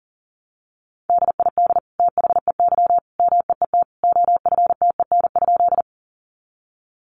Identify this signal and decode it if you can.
Morse code, heard as a keyed tone, about 40 wpm, 700 Hz. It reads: DID THEY MEET OFTEN?
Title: Sentences from Top 1100 Words 40wpm